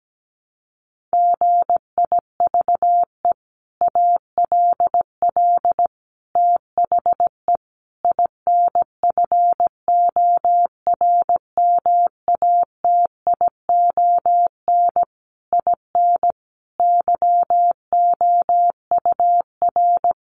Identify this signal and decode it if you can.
Morse code, heard as a keyed tone, about 17 wpm, 700 Hz. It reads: GIVE ALL THE INFORMATION IN YOUR
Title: Sentences from Top 600 Words 17wpm